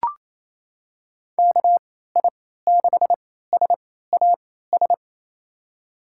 Morse code, heard as a keyed tone, about 28 wpm, spaced as for 16 wpm, 700 Hz. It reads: KI6SAS
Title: USA Call Signs - 3x Character Spacing - 28wpm